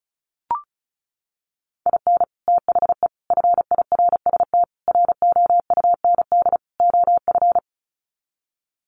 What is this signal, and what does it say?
Morse code, heard as a keyed tone, about 34 wpm, 700 Hz. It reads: IN THE FIRST ROUND OF
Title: Sets of 5 Words Encore 35wpm